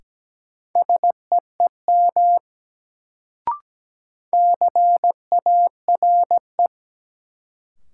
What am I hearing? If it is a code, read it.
Morse code, heard as a keyed tone, about 17 wpm, 700 Hz. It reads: SEEM CARE
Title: Sentences from Top 300 Words - New Word Drill - 17wpm